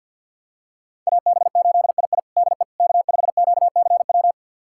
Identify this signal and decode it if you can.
Morse code, heard as a keyed tone, about 53 wpm, 700 Hz. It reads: AB8II DE K5QCW
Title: USA Callsign DE Callsign - 5 Characters - 50wpm